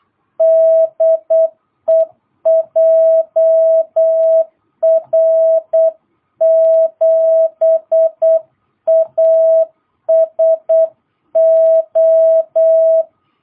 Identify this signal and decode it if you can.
Morse code, heard as a keyed tone, about 8 wpm, 660 Hz. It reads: DE JR7ASO
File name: jh8cbh / 2022cw